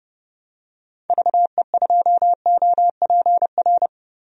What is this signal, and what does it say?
Morse code, heard as a keyed tone, about 30 wpm, 700 Hz. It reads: VE2OPR
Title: DXCC-100 Call Signs - 6 Characters - 30wpm